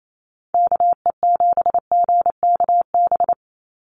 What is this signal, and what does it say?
Morse code, heard as a keyed tone, about 28 wpm, 700 Hz. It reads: KE7GKB
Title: USA Call Signs 28wpm